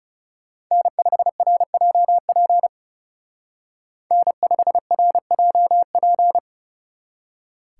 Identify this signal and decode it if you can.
Morse code, heard as a keyed tone, about 35 wpm, 700 Hz. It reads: N5RJP N5RJP
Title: Speed Racing - USA 1x3 Call Signs 30wpm